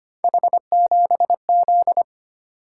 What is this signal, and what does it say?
Morse code, heard as a keyed tone, about 25 wpm, 700 Hz. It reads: H7Z
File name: Three Letter-Number Combinations - 25wpm